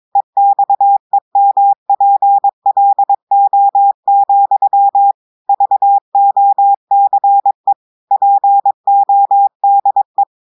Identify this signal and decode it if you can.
Morse code, heard as a keyed tone, about 22 wpm, 800 Hz. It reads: EXEMPLO, VOCE PODE
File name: audios / portugais / portugais texte en clair 2 vitesse 22 mots minutes